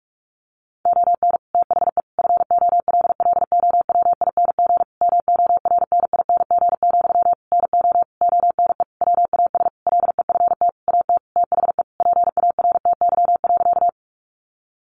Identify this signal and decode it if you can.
Morse code, heard as a keyed tone, about 44 wpm, 700 Hz. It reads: ON THE FOLLOWING MORNING, NO ONE WAS LEFT AT THE PARTY.
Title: Sentences from Top 1400 Words 45wpm